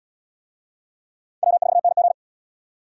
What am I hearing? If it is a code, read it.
Morse code, heard as a keyed tone, about 77 wpm, 700 Hz. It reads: W4NG